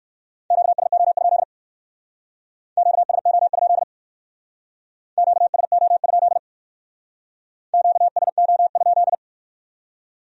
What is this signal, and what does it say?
Morse code, heard as a keyed tone, about 56 wpm, 700 Hz. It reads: QSO? QSO? QSO? QSO?